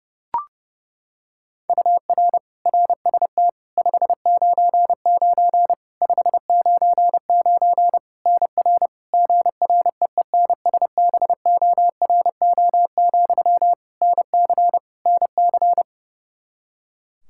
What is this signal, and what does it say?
Morse code, heard as a keyed tone, about 30 wpm, 700 Hz. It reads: UR RST 599 599 NR GREENSBORO, NC NC
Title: Practice QSO - Exchange 1 - Beginning - 30wpm